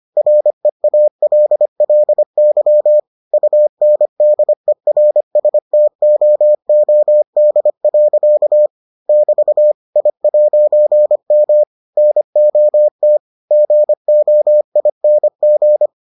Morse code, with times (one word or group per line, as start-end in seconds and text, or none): 0.2-3.0 REALLY
3.3-8.7 UNDERSTOOD.
9.1-9.7 <BT>
10.0-11.6 I'M
12.0-13.2 NOT
13.5-15.9 GOING